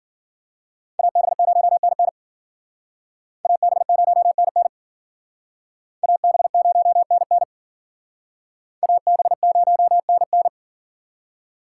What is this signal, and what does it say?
Morse code, heard as a keyed tone, about 51 wpm, 700 Hz. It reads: AB0NN AB0NN AB0NN AB0NN